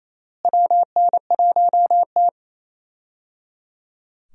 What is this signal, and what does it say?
Morse code, heard as a keyed tone, about 27 wpm, 700 Hz. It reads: WN1T
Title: USA Call Signs - 4 Characters - Repeated 3x - 28wpm